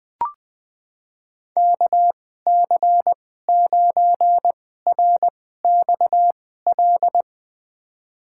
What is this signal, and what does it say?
Morse code, heard as a keyed tone, about 20 wpm, spaced as for 14 wpm, 700 Hz. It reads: KC9RXL
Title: USA Call Signs - 6 Characters - 2x Character Spacing - 20wpm